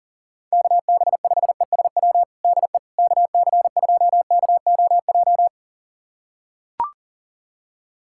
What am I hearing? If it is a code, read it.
Morse code, heard as a keyed tone, about 40 wpm, 700 Hz. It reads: KB5ESW DE KC2KOJ
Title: USA Callsign DE Callsign 40wpm